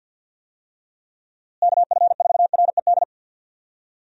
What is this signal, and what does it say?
Morse code, heard as a keyed tone, about 52 wpm, 700 Hz. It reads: KF4RED
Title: USA Call Signs - 6 Characters - Repeated 3x - 50wpm